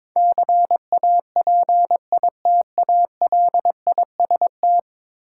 Morse code, heard as a keyed tone, about 22 wpm, 700 Hz. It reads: CAPITALIST
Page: morse-code-ninja-practice-22wpm